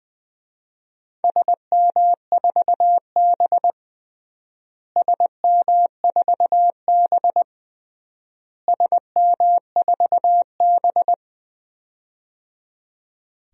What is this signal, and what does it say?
Morse code, heard as a keyed tone, about 21 wpm, 700 Hz. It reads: SM4B SM4B SM4B
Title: DXCC-100 Call Signs - 4 Characters - Repeated 3x - 20wpm